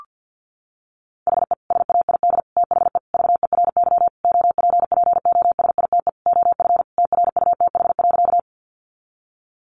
Morse code, heard as a keyed tone, about 50 wpm, 700 Hz. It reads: HE SAID THE VERY OPPOSITE OF TRUTH.
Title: Sentences from Top 1300 Words 50wpm